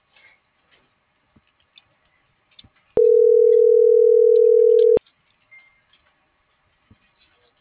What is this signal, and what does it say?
Ringback tone (North America), 440 + 480 Hz.